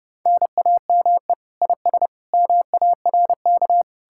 Morse code, heard as a keyed tone, about 40 wpm, 700 Hz. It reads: NAME IS MARK